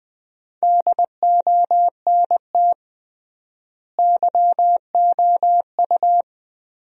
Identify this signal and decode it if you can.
Morse code, heard as a keyed tone, about 21 wpm, 700 Hz. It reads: DONT YOU
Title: 3x Word-Spacing - Sentences from Top 300 Words - 20wpm